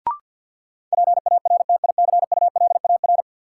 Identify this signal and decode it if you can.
Morse code, heard as a keyed tone, about 50 wpm, 700 Hz. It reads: PARTICULAR
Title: Sentences from Top 600 Words - New Word Drill - 50wpm